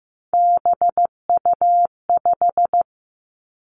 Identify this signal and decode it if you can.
Morse code, heard as a keyed tone, about 15 wpm, 700 Hz. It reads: BU5
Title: Three Challenging Letter-Number Combinations 15wpm